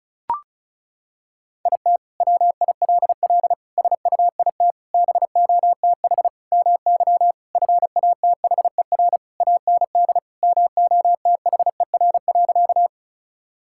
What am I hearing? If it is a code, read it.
Morse code, heard as a keyed tone, about 36 wpm, 700 Hz. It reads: IT WILL SUIT BOTH MY FATHER AND MOTHER.
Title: Sentences from Top 1600 Words 35wpm